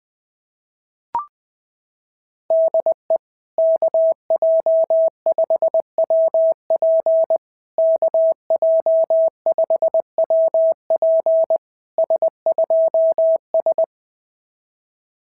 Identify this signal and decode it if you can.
Morse code, heard as a keyed tone, about 20 wpm, 650 Hz. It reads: DE KJ5WP KJ5WP S2S
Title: SOTA QSO Practice 20wpm